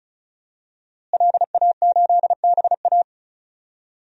Morse code, heard as a keyed tone, about 36 wpm, 700 Hz. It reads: LA8BA
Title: DXCC-100 Call Signs - 5 Characters - Repeated 2x - 35wpm